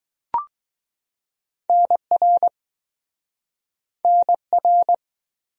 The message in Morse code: NR NR

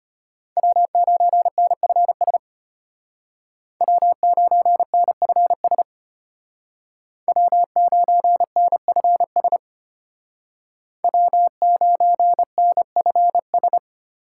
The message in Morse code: W9NFS W9NFS W9NFS W9NFS

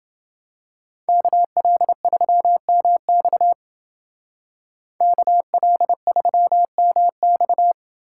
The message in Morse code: KL3MX KL3MX

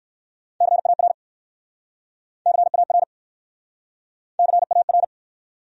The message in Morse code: CAR CAR CAR

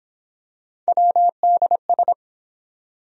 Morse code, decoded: WDS

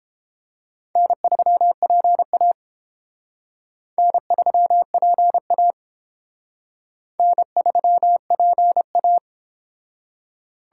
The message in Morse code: N3PA N3PA N3PA

